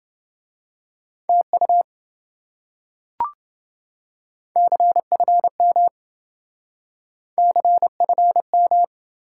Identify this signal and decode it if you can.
Morse code, heard as a keyed tone, about 28 wpm, 700 Hz. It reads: TU CFM CFM